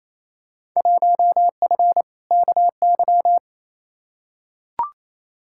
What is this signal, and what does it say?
Morse code, heard as a keyed tone, about 28 wpm, 700 Hz. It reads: 1F KY
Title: Field Day Exchanges Encore - 28wpm